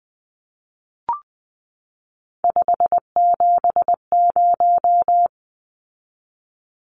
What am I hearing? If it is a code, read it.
Morse code, heard as a keyed tone, about 20 wpm, 700 Hz. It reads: 570
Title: Numbers - 3-Digit 20wpm